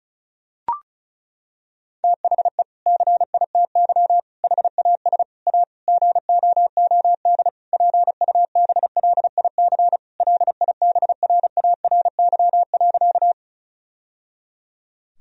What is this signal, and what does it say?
Morse code, heard as a keyed tone, about 34 wpm, 700 Hz. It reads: THE CITY HAS A GOOD PUBLIC LIBRARY.
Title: Sentences from Top 1000 Words - Review Part 4 - 35wpm